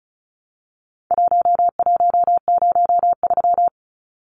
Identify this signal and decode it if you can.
Morse code, heard as a keyed tone, about 34 wpm, 700 Hz. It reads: 1103
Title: Numbers - Contest Style - 4-Digit 35wpm